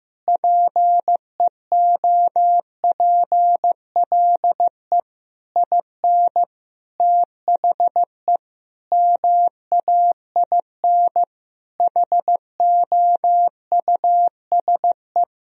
Morse code, heard as a keyed tone, about 15 wpm, 700 Hz. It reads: PEOPLE IN THE MAIN HOUSE
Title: Sentences from Top 1000 Words - Review Part 2 - 15wpm